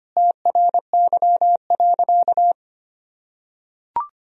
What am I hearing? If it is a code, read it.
Morse code, heard as a keyed tone, about 25 wpm, 700 Hz. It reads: TRY.
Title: Sentences from Top 1000 Words - Review Part 4 - 25wpm